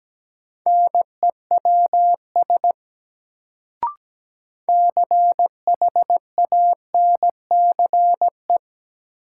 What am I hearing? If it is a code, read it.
Morse code, heard as a keyed tone, about 17 wpm, 700 Hz. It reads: NEWS CHANCE
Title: Sentences from Top 800 Words - New Word Drill - 17wpm